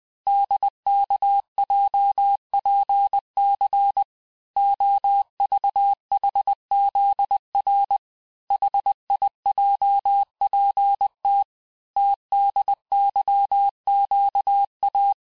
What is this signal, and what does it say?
Morse code, heard as a keyed tone, about 20 wpm, 800 Hz. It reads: DKJPC OVHZR HIJPT TDYQA
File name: cours-1584-teste-chiffre-vitesse-19mpm